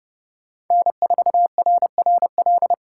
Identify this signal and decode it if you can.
Morse code, heard as a keyed tone, about 29 wpm, 700 Hz. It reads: N4RRL
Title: USA Callsign DE Callsign - 5 Characters - 30wpm